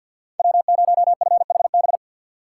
Morse code, heard as a keyed tone, about 48 wpm, 700 Hz. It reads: W0FHB
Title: USA Call Signs - 2x Character Spacing - 50wpm